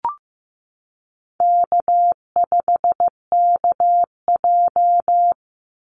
Morse code, heard as a keyed tone, about 15 wpm, 700 Hz. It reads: K5KJ
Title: Easy Call Signs - 15wpm